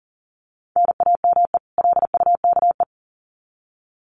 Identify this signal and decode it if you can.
Morse code, heard as a keyed tone, about 40 wpm, 700 Hz. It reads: NAME LUKE
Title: Common QSO Phrases 40wpm